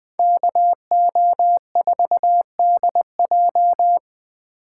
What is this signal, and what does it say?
Morse code, heard as a keyed tone, about 19 wpm, 700 Hz. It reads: KO4DJ
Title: Speed Racing - USA 2x2 Call Signs 17wpm